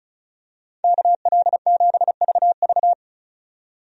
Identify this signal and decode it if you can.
Morse code, heard as a keyed tone, about 36 wpm, 700 Hz. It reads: KL7VV
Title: USA Call Signs - 5 Characters - 35wpm